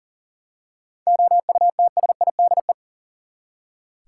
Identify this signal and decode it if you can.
Morse code, heard as a keyed tone, about 40 wpm, 700 Hz. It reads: OUTSIDE
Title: Seven Letter Words - 40wpm